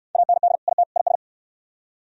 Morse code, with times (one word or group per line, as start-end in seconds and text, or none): 0.2-0.6 AND
0.7-0.8 IT
1.0-1.2 IS